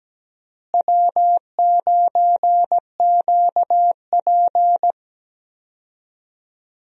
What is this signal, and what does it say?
Morse code, heard as a keyed tone, about 17 wpm, 700 Hz. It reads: W9QP
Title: USA Call Signs - 4 Characters - Repeated 3x - 17wpm